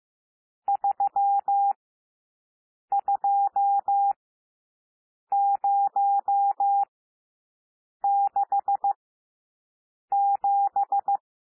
Morse code, heard as a keyed tone, about 15 wpm, spaced as for 6 wpm, 800 Hz. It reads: 32067
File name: cwmp3 / N1 SOUND NUMBERS 15 wpm